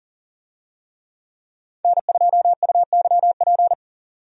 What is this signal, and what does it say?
Morse code, heard as a keyed tone, about 37 wpm, 700 Hz. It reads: N2UYP
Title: USA Call Signs - 5 Characters - 40wpm